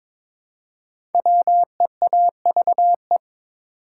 Morse code, heard as a keyed tone, about 23 wpm, 700 Hz. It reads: WEAVE
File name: 5 Letter Words - First 3 Letters Given - 22wpm